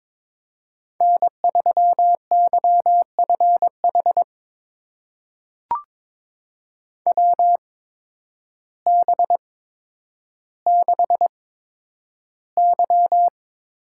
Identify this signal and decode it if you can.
Morse code, heard as a keyed tone, about 22 wpm, spaced as for 10 wpm, 700 Hz. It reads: N3YFH WB6Y
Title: USA Call Signs - 8x Character Spacing - 22wpm